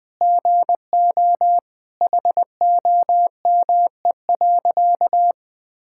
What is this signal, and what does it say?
Morse code, heard as a keyed tone, about 19 wpm, 700 Hz. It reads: GO HOME.
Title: Sentences with 4-Word Phrases - Spoken Beginning - 20wpm